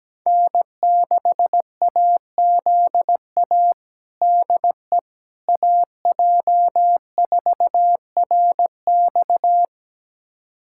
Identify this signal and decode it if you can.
Morse code, heard as a keyed tone, about 17 wpm, 700 Hz. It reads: N6AZA DE AJ4RX
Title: USA Callsign DE Callsign - 5 Characters - 17wpm